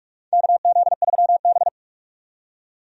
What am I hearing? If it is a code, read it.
Morse code, heard as a keyed tone, about 46 wpm, 700 Hz. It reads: KZ3B
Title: USA Call Signs - 4 Characters - 45wpm